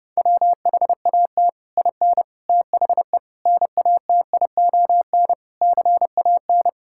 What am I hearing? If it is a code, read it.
Morse code, heard as a keyed tone, about 30 wpm, 700 Hz. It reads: WHAT IN THE NATION CAN
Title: Sentences from Top 1000 Words - Review Part 6 - 30wpm